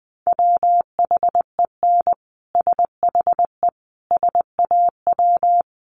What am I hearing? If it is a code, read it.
Morse code, heard as a keyed tone, about 20 wpm, 700 Hz. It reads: WHEN SHE SAW
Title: Sentences from Top 1200 Words 20wpm